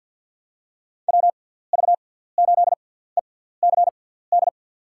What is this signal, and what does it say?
Morse code, heard as a keyed tone, about 50 wpm, spaced as for 17 wpm, 700 Hz. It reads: WV8ECD
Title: USA Call Signs - 6 Characters - 6x Character Spacing - 50wpm